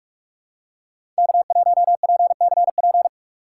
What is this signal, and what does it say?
Morse code, heard as a keyed tone, about 44 wpm, 700 Hz. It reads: K1PCP